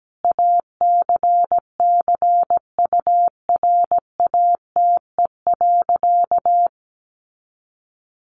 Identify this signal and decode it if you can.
Morse code, heard as a keyed tone, about 17 wpm, 700 Hz. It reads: ACCURATE.
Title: Sentences from Top 1200 Words 17wpm